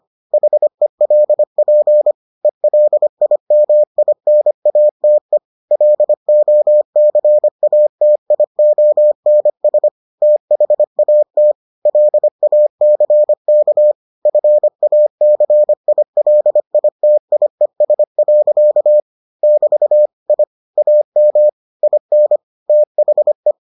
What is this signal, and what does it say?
Morse code, heard as a keyed tone, about 25 wpm, 600 Hz. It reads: HELP ELIMINATE LOCATIONS THAT LACK FACILITIES. <BT> I AM IN THE